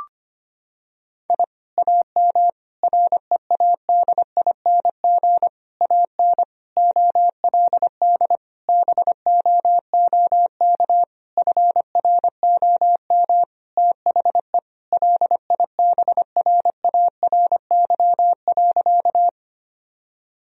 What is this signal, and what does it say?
Morse code, heard as a keyed tone, about 25 wpm, 700 Hz. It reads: I AM READING AN OLD BOOK FROM THE LIBRARY.
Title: Sentences from Top 1000 Words - Review Part 4 - 25wpm